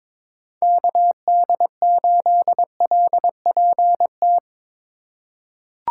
Morse code, heard as a keyed tone, about 22 wpm, 700 Hz. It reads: KD8LPT
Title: USA Call Signs - 6 Characters - 3x Character Spacing - 22wpm